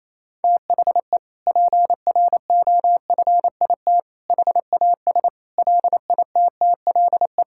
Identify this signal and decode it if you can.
Morse code, heard as a keyed tone, about 27 wpm, 700 Hz. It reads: THE PROFIT HAS LITTLE
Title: Sentences from Top 1000 Words - Review Part 5 - 28wpm